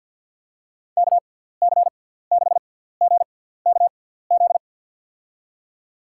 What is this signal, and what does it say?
Morse code, heard as a keyed tone, about 53 wpm, 700 Hz. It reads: K C 6 G K Z
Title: USA Call Signs - 6 Characters - 6x Character Spacing - 50wpm